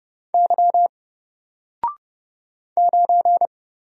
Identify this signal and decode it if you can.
Morse code, heard as a keyed tone, about 30 wpm, 700 Hz. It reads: Y 9
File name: Single-Letter-Number-30wpm